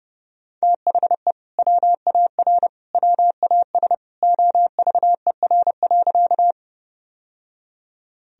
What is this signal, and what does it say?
Morse code, heard as a keyed tone, about 32 wpm, 700 Hz. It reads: THE WAR WAS OVER.
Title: Sentences from Top 1000 Words - Review Part 2 - 30wpm